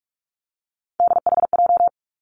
Morse code, heard as a keyed tone, about 46 wpm, 700 Hz. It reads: DHJ
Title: Three Challenging Letter-Number Combinations 45wpm